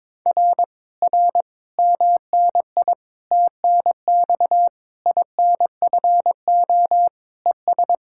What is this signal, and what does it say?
Morse code, heard as a keyed tone, about 22 wpm, 700 Hz. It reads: R R MNI TNX INFO ES
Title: Speed-Racing Common QSO Phrases Encore - 22wpm